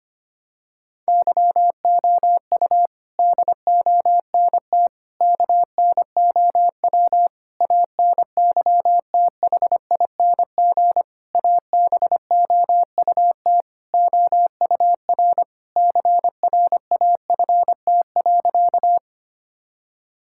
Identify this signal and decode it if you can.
Morse code, heard as a keyed tone, about 25 wpm, 700 Hz. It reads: YOU DONT KNOW ANYTHING ABOUT OUR CRAFT.
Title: Sentences from Top 1000 Words - Review Part 6 - 25wpm